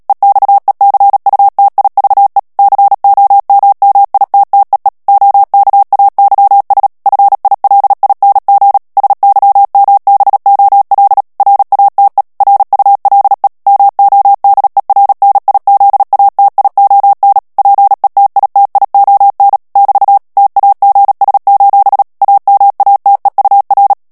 Morse code, heard as a keyed tone, about 37 wpm, 800 Hz. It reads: EXECUTIVE COMMITTEE OKAYS FILING SYMBOL RATE RULE MODERNIZATION PETITION <BT> TAGS: AMATEUR
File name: morse mp3 / ANGLAIS texte clair 3 vitesse 35 mots minute